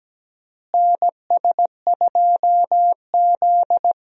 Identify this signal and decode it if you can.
Morse code, heard as a keyed tone, about 17 wpm, 700 Hz. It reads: NS2Z
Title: USA Call Signs - 4 Characters - Repeated 2x - 17wpm